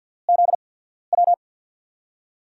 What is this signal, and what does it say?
Morse code, heard as a keyed tone, about 52 wpm, spaced as for 14 wpm, 700 Hz. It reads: ZW
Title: USA Call Signs - 6 Characters - 8x Character Spacing - 50wpm